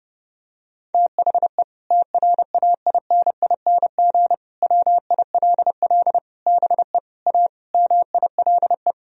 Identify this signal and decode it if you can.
Morse code, heard as a keyed tone, about 32 wpm, 700 Hz. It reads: THE TRAINING WILL BE A MILE
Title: Sentences from Top 1000 Words - Review Part 4 - 30wpm